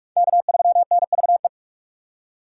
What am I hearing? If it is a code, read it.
Morse code, heard as a keyed tone, about 44 wpm, 700 Hz. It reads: K3NVE